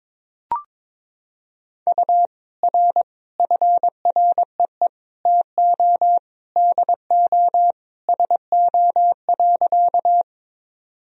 Morse code, heard as a keyed tone, about 22 wpm, 700 Hz. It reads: U R FREE TO DO SO.
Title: Sentences from Top 500 Words and Abbreviations 22wpm